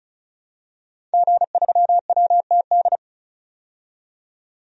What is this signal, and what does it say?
Morse code, heard as a keyed tone, about 34 wpm, 700 Hz. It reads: G3WTD
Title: DXCC-100 Call Signs - 5 Characters - 35wpm